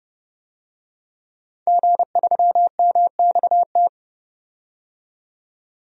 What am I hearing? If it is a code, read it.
Morse code, heard as a keyed tone, about 30 wpm, 700 Hz. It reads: G3MXT
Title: DXCC-100 Call Signs - 5 Characters - 30wpm